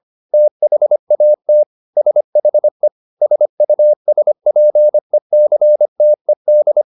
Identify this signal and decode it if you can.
Morse code, heard as a keyed tone, about 25 wpm, 600 Hz. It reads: THAT SHE SUSPECTED